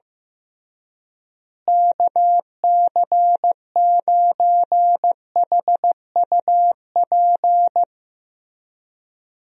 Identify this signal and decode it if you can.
Morse code, heard as a keyed tone, about 15 wpm, 700 Hz. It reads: KC9HUP
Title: USA Call Signs - 6 Characters - Repeated 3x - 15wpm